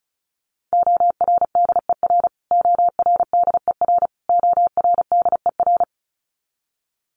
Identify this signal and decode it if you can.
Morse code, heard as a keyed tone, about 34 wpm, 700 Hz. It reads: ORDER ORDER ORDER
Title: Top 500 Words Repeated 35wpm